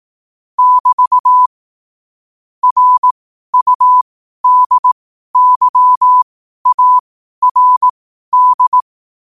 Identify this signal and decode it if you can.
Morse code, heard as a keyed tone, about 18 wpm, spaced as for 12 wpm, 1000 Hz. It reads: <BT> RUDYARD
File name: qotdfiles / 12wpm